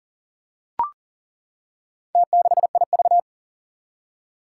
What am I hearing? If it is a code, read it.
Morse code, heard as a keyed tone, about 37 wpm, 700 Hz. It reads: T6IV
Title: Four Challenging Letter-Number Combinations 40wpm